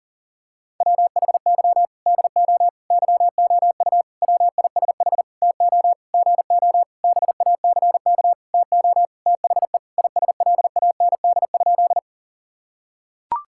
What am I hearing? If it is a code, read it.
Morse code, heard as a keyed tone, about 40 wpm, 700 Hz. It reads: WHY DO YOU WISH TO GO BACK TO THE ISLAND?
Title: Sentences from Top 1600 Words 40wpm